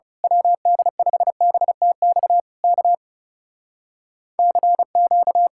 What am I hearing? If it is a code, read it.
Morse code, heard as a keyed tone, about 28 wpm, 700 Hz. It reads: WD5BTX K CQ
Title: Speed-Racing Common QSO Phrases - 30wpm